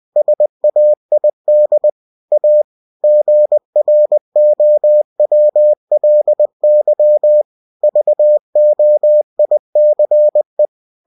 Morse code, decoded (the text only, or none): SAID A GROWLY VOICE